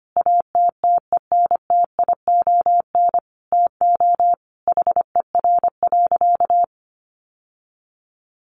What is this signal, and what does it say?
Morse code, heard as a keyed tone, about 25 wpm, 700 Hz. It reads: ATTENTION TO HER.
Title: Sentences from Top 1000 Words 25wpm